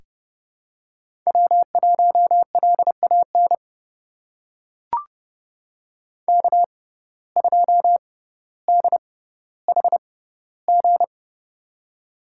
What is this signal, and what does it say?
Morse code, heard as a keyed tone, about 30 wpm, spaced as for 15 wpm, 700 Hz. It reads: W1LAN K2DHG
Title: USA Call Signs - 5 Characters - 6x Character Spacing - 30wpm